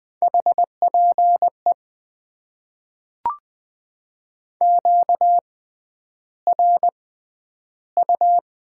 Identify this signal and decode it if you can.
Morse code, heard as a keyed tone, about 20 wpm, spaced as for 10 wpm, 700 Hz. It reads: HPE QRU
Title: Top 90 QSO Elements - 6x Character Spacing - 20wpm